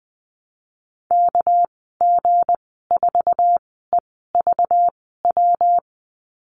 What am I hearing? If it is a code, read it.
Morse code, heard as a keyed tone, about 20 wpm, spaced as for 14 wpm, 700 Hz. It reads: KG4EVW
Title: 231 Lesson 37g - Letter x - Call Signs 20wpm